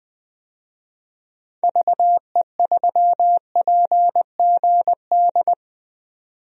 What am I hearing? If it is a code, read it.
Morse code, heard as a keyed tone, about 21 wpm, 700 Hz. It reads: VE3PGD
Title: DXCC-100 Call Signs - 6 Characters - 20wpm